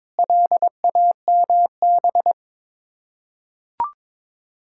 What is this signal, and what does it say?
Morse code, heard as a keyed tone, about 22 wpm, 690 Hz. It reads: LAMB
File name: Difficult Words with Hints - 22wpm